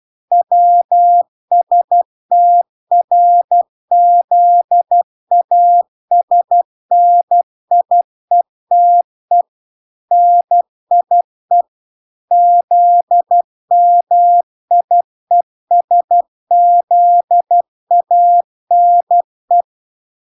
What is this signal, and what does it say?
Morse code, heard as a keyed tone, about 12 wpm, 700 Hz. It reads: WSTRZASNIETE NIE ZMIESZANE